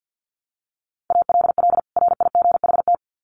Morse code, heard as a keyed tone, about 50 wpm, 700 Hz. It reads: ALL RIGHT
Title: Sets of 2 Words 50wpm